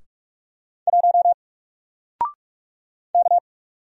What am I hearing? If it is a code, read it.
Morse code, heard as a keyed tone, about 46 wpm, 700 Hz. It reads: 1 K
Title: Single Character (Letter:Number:Punctuation:Prosign) Alt 45wpm